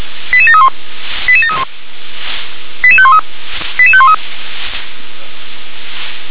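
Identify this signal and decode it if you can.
Five-tone selective call (ZVEI), heard as quick runs of five tones, tones of about 70 ms. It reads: 88731 88731 88513 88513
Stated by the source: ZVEI five-tone call received with an SDR.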